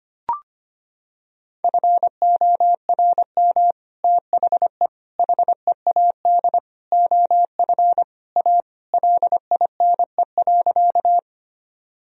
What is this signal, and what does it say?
Morse code, heard as a keyed tone, about 25 wpm, 700 Hz. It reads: FORM THE HEAD OF A LINE.
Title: Sentences from Top 1000 Words - Review Part 1 - 25wpm